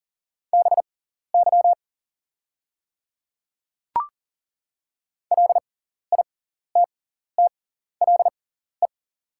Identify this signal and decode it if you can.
Morse code, heard as a keyed tone, about 40 wpm, spaced as for 14 wpm, 690 Hz. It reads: BY LITTLE
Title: 6x Character Spacing - Top 200 Words - 40wpm